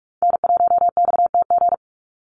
Morse code, heard as a keyed tone, about 44 wpm, 700 Hz. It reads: N1XTG
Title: USA Call Signs 45wpm